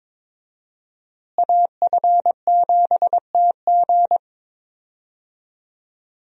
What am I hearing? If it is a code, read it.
Morse code, heard as a keyed tone, about 22 wpm, 700 Hz. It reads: AF7TG
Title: USA Call Signs - 5 Characters - Repeated 3x - 22wpm